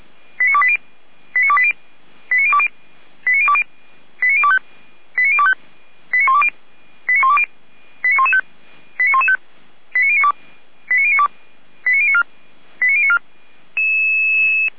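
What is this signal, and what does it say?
Five-tone selective call (ZVEI), heard as quick runs of five tones, tones of about 70 ms. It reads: 89280 89280 89020 89020 89025 89025 89120 89120 89115 89115 89092 89092 89094 89094
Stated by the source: ZVEI five-tone selective call received via SDR, fire brigade feuerwehr-orlamuende.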